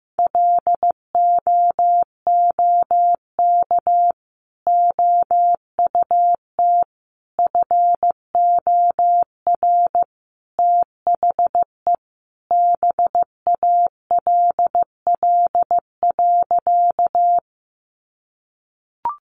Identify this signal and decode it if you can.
Morse code, heard as a keyed tone, about 15 wpm, 700 Hz. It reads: LOOK OUT FOR THE BALL.
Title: Sentences from Top 900 Words 15wpm